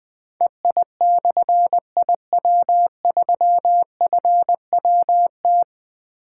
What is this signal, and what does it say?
Morse code, heard as a keyed tone, about 21 wpm, 700 Hz. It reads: EI/IW3FWT
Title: Difficult Call Signs 20wpm